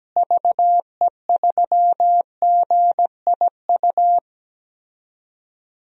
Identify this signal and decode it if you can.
Morse code, heard as a keyed tone, about 17 wpm, 700 Hz. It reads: VE3GIU
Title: DXCC-100 Call Signs - 6 Characters - Repeated 3x - 17wpm